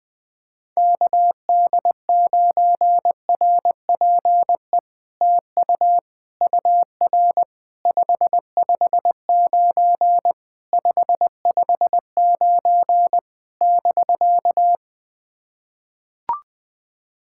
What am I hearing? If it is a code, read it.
Morse code, heard as a keyed tone, about 20 wpm, 700 Hz. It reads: KD9RPE TU UR 559 559 <BK>
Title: POTA QSO Practice - 20wpm